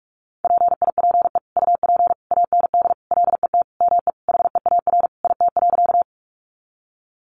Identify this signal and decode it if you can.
Morse code, heard as a keyed tone, about 44 wpm, 700 Hz. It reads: PIPE UP AND LET ME HEAR IT.
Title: Sentences from Top 1500 Words 45wpm